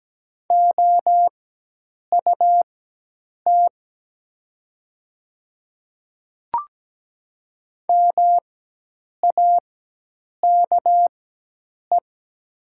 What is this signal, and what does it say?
Morse code, heard as a keyed tone, about 17 wpm, spaced as for 8 wpm, 700 Hz. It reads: OUT MAKE
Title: 4x Character Spacing - Top 100 Words - 17wpm